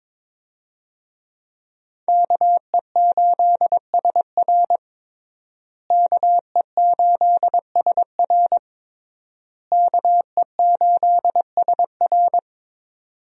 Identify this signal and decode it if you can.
Morse code, heard as a keyed tone, about 22 wpm, 700 Hz. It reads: KE8SR KE8SR KE8SR
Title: USA Call Signs - 5 Characters - Repeated 3x - 22wpm